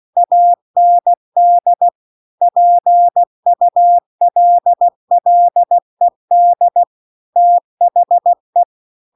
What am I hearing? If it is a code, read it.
Morse code, heard as a keyed tone, about 16 wpm, 700 Hz. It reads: AND PULLED THE
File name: CW-The-Big-Twoheaded-River-2-16WPM